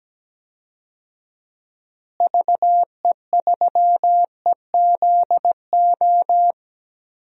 Morse code, heard as a keyed tone, about 17 wpm, 700 Hz. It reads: VE3EZO
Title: DXCC-100 Call Signs - 6 Characters - Repeated 3x - 17wpm